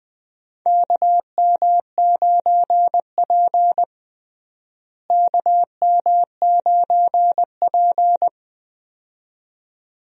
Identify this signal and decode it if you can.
Morse code, heard as a keyed tone, about 21 wpm, 700 Hz. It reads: KM9P KM9P
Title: USA Call Signs - 4 Characters - Repeated 2x - 20wpm